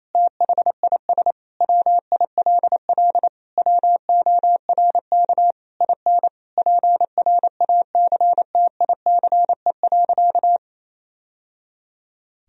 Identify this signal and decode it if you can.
Morse code, heard as a keyed tone, about 28 wpm, 700 Hz. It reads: THIS WILL WORK IN PRACTICE.
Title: Sentences from Top 1000 Words - Review Part 4 - 28wpm